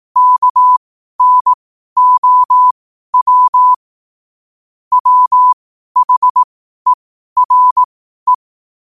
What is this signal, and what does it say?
Morse code, heard as a keyed tone, about 18 wpm, spaced as for 12 wpm, 1000 Hz. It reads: KNOW WHERE
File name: qotdfiles / 12wpm